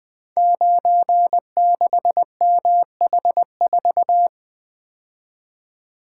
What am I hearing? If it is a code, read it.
Morse code, heard as a keyed tone, about 20 wpm, 700 Hz. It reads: 96MH4
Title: Code Groups - 5 Letter:Numbers - 20wpm